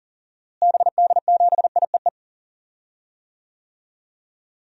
Morse code, heard as a keyed tone, about 40 wpm, 700 Hz. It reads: BD7IEE
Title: DXCC-100 Call Signs - 6 Characters - Repeated 2x - 40wpm